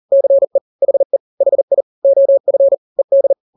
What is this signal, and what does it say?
Morse code, heard as a keyed tone, about 40 wpm, 550 Hz. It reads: CE HE HI OF ED
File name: n-grams / 2-grams 40 35 WPM